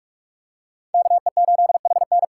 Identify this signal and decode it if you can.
Morse code, heard as a keyed tone, about 42 wpm, 690 Hz. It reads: KE8HN
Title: USA Call Signs - 5 Characters - 2x Character Spacing - 45wpm